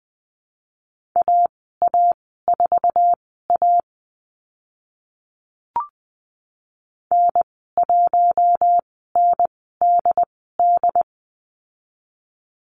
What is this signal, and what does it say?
Morse code, heard as a keyed tone, about 20 wpm, spaced as for 15 wpm, 700 Hz. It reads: AA4A N1NDD
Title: 069 Lesson 12g - Letter d - Call Signs 20wpm